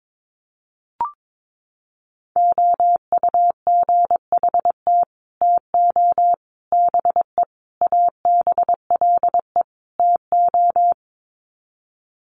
Morse code, heard as a keyed tone, about 22 wpm, 700 Hz. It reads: OUGHT TO BE ABLE TO
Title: Sets of 5 Words Encore2 22wpm